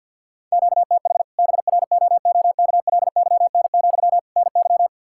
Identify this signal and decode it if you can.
Morse code, heard as a keyed tone, about 53 wpm, 700 Hz. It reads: QTH BROOKLYN, NY
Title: Practice QSO - Exchange 1 - Beginning - 50wpm